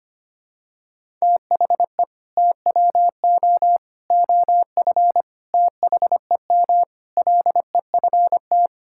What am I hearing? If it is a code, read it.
Morse code, heard as a keyed tone, about 25 wpm, 700 Hz. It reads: THE TWO OF THEM LEFT
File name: practice / 25wpm / Sentences with 4-Word Phrases - 25wpm